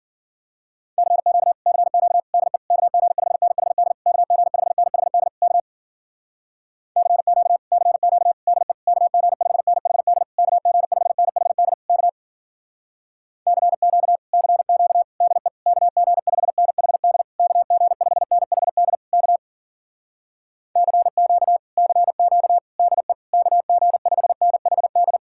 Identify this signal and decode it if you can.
Morse code, heard as a keyed tone, about 52 wpm, 700 Hz. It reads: CQ CQ DE KG5NHD KG5NHD K CQ CQ DE KG5NHD KG5NHD K CQ CQ DE KG5NHD KG5NHD K CQ CQ DE KG5NHD